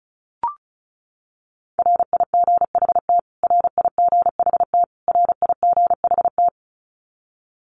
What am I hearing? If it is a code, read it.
Morse code, heard as a keyed tone, about 34 wpm, 700 Hz. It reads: RIGHT RIGHT RIGHT
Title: Top 300 Words Repeated 35wpm